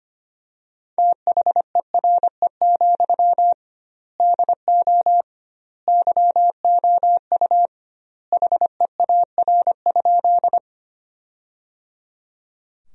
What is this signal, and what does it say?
Morse code, heard as a keyed tone, about 25 wpm, 700 Hz. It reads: THERE, DO YOU HEAR?
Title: 2x Word-Spacing - Sentences from Top 300 Words - 25wpm